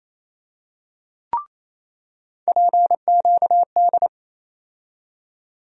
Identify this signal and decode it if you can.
Morse code, heard as a keyed tone, about 28 wpm, 700 Hz. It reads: PQD
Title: USA Call Sign Postfixes - 3 Letters 28wpm